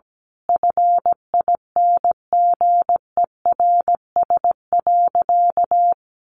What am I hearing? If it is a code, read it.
Morse code, heard as a keyed tone, about 17 wpm, 700 Hz. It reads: FINGERS.
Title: A Sound Of Thunder 17wpm